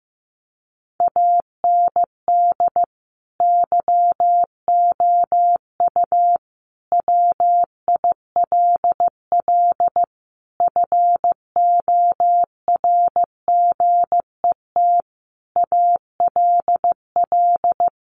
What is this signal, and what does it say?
Morse code, heard as a keyed tone, about 15 wpm, 700 Hz. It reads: AND YOU WILL FORGET ALL
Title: Sentences from Top 1200 Words 15wpm